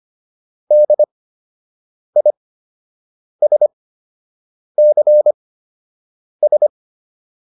Morse code, heard as a keyed tone, about 25 wpm, spaced as for 7 wpm, 600 Hz. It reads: DISCS